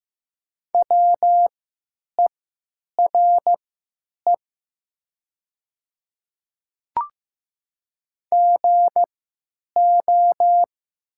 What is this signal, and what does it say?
Morse code, heard as a keyed tone, about 15 wpm, spaced as for 9 wpm, 700 Hz. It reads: WERE GO
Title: 3x Character Spacing - Top 100 Words - 15wpm